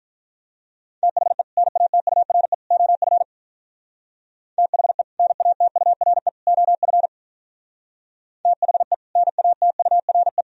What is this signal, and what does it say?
Morse code, heard as a keyed tone, about 42 wpm, 700 Hz. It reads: THE NATURE OF THE NATURE OF THE NATURE
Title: Speed-Racing - Sets of 3 Words 35wpm